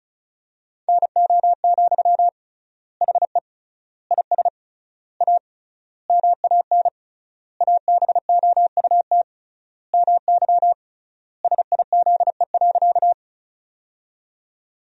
Morse code, heard as a keyed tone, about 34 wpm, 700 Hz. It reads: NO, HE IS A MAN ABOUT MY SIZE.
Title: 3x Word-Spacing - Sentences from Top 400 Words - 35wpm